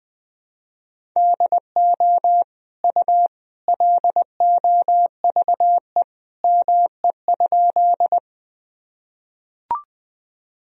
Morse code, heard as a keyed tone, about 20 wpm, 700 Hz. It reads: DO U LOVE ME?
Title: Sentences from Top 400 Words and Abbreviations 20wpm